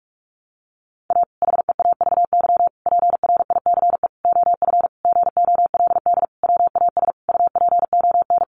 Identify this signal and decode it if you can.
Morse code, heard as a keyed tone, about 44 wpm, 700 Hz. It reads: A HEAVY PRICE OF GOLD WAS UPON
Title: Sentences from Top 1100 Words 45wpm